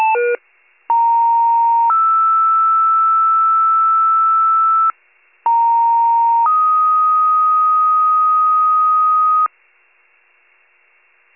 Two-tone page, heard as a one-second tone, then a three-second tone, about 930 then 1320 Hz, about 930 then 1250 Hz. It is heard from 0.9 to 9.5 s.
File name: pager / 82 FIRE / wcalert